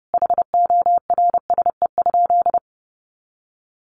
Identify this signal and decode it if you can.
Morse code, heard as a keyed tone, about 28 wpm, 700 Hz. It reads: HORSE?
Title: Sentences from Top 500 Words 30wpm